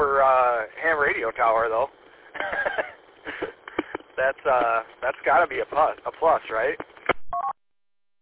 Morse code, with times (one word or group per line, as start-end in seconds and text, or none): none